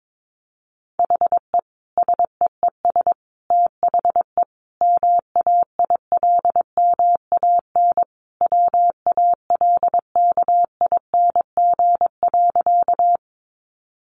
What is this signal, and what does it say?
Morse code, heard as a keyed tone, about 22 wpm, 700 Hz. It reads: HE SEES THE MAILMAN WALKING.
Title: Additional Easy Phrases 22wpm